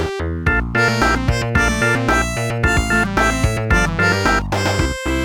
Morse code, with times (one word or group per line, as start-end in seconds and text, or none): none